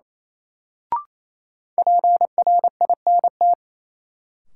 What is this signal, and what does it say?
Morse code, heard as a keyed tone, about 28 wpm, 700 Hz. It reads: PRINT
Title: Sentences from Top 1400 Words - New Word Drill Alt - 28wpm